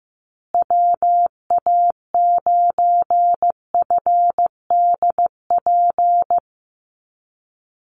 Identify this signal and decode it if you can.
Morse code, heard as a keyed tone, about 15 wpm, 700 Hz. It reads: WA9FDP
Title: USA Call Signs Repeated 2x 15wpm